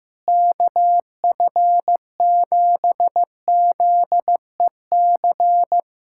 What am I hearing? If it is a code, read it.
Morse code, heard as a keyed tone, about 15 wpm, 700 Hz. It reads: KF7ZEC
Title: USA Callsign DE Callsign - 6 Characters - 15wpm